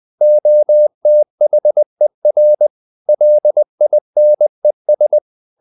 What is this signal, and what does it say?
Morse code, heard as a keyed tone, about 20 wpm, 600 Hz. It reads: OTHER LINES
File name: the old man and the sea / the old man and the sea part 1 20wpm